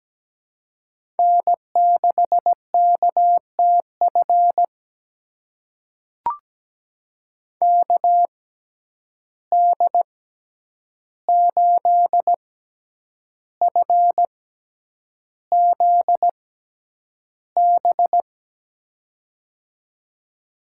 Morse code, heard as a keyed tone, about 17 wpm, spaced as for 8 wpm, 700 Hz. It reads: N6KTF KD8FZB
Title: USA Call Signs - 6x Character Spacing - 17wpm